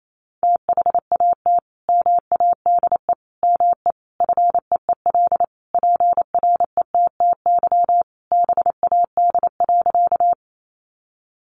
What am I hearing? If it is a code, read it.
Morse code, heard as a keyed tone, about 28 wpm, 700 Hz. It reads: THAT MADE ME FEEL PRETTY BAD.
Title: Sentences from Top 1000 Words 28wpm